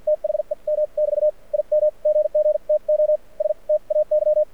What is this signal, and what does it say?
Morse code, heard as a keyed tone, about 53 wpm, spaced as for 42 wpm, 600 Hz, with radio noise. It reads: THEM <BT> IM GG TO STAY
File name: archived / CRSnet